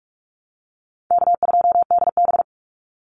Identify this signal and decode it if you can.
Morse code, heard as a keyed tone, about 44 wpm, 700 Hz. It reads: K2DB
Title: Easy Call Signs - 45wpm